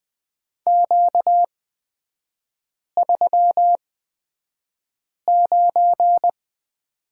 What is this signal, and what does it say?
Morse code, heard as a keyed tone, about 20 wpm, spaced as for 5 wpm, 700 Hz. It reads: Q39
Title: 250 Lesson 40e - BK - Three Characters 20wpm